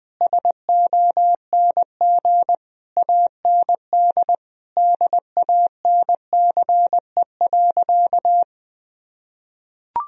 Morse code, heard as a keyed tone, about 21 wpm, 700 Hz. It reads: SONG AND DANCE.
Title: Sentences from Top 1000 Words - Review Part 4 - 20wpm